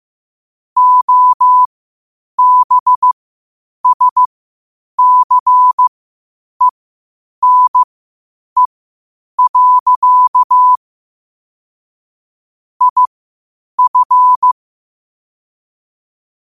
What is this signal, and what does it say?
Morse code, heard as a keyed tone, about 15 wpm, spaced as for 8 wpm, 1000 Hz. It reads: OBSCENE. IF
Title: 8 WPM morse code quotes for Fri, 01 Mar 2024